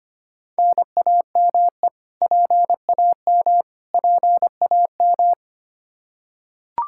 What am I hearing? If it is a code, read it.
Morse code, heard as a keyed tone, about 25 wpm, 700 Hz. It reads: NAME PAM PAM
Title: Practice QSO - Exchange 1 - Beginning - 25wpm